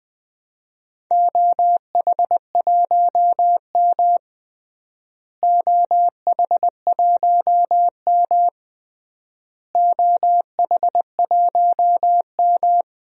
Morse code, heard as a keyed tone, about 20 wpm, 690 Hz. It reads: OH1M OH1M OH1M
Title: DXCC-100 Call Signs - 4 Characters - Repeated 3x - 20wpm